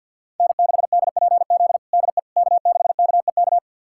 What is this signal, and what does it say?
Morse code, heard as a keyed tone, about 52 wpm, 700 Hz. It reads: N6DPZ DE K6KEK
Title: USA Callsign DE Callsign - 5 Characters - 50wpm